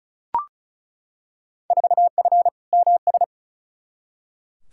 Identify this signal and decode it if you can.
Morse code, heard as a keyed tone, about 34 wpm, 700 Hz. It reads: 4F MS
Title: Field Day Exchanges Encore - 35wpm